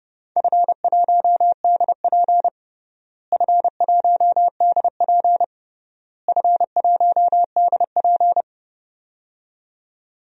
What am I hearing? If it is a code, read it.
Morse code, heard as a keyed tone, about 31 wpm, 700 Hz. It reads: F1DP F1DP F1DP
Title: DXCC-100 Call Signs - 4 Characters - Repeated 3x - 30wpm